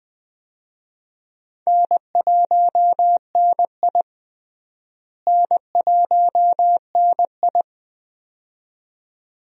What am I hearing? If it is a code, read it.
Morse code, heard as a keyed tone, about 20 wpm, 700 Hz. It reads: N1NI N1NI
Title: USA Call Signs - 4 Characters - Repeated 2x - 20wpm